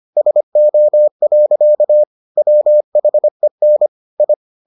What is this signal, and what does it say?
Morse code, heard as a keyed tone, about 25 wpm, 600 Hz. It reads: SO. WHEN I